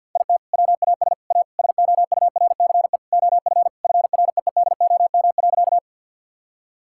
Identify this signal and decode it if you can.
Morse code, heard as a keyed tone, about 46 wpm, 700 Hz. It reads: IT WAS A SOURCE OF FREEDOM.
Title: Sentences from Top 1000 Words - Review Part 5 - 50wpm